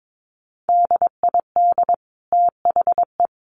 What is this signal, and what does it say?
Morse code, heard as a keyed tone, about 22 wpm, 700 Hz. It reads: DID THE
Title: Sentences from Top 800 Words 22wpm